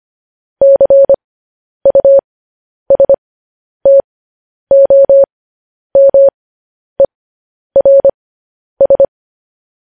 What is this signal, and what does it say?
Morse code, heard as a keyed tone, about 25 wpm, spaced as for 10 wpm, 550 Hz. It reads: CUSTOMERS
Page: news-headlines-in-morse-code-at-10-wpm